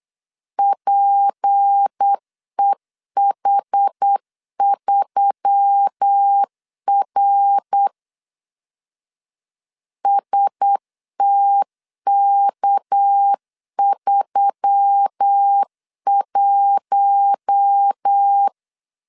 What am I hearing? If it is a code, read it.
Morse code, heard as a keyed tone, about 8 wpm, 800 Hz. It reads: PEH3R STK31
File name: LEZIONI / 6 PKX0 / 6 PK 8wpm 3003